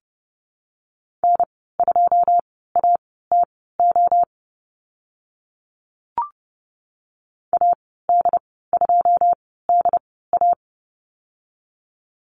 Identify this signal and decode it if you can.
Morse code, heard as a keyed tone, about 30 wpm, spaced as for 17 wpm, 700 Hz. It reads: N2ATO AD2DA
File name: Lesson 14g - Number 2 - Call Signs 30wpm